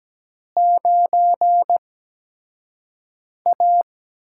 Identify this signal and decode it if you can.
Morse code, heard as a keyed tone, about 17 wpm, spaced as for 5 wpm, 700 Hz. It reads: 9A
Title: USA Call Signs - 4 Characters - 8x Character Spacing - 17wpm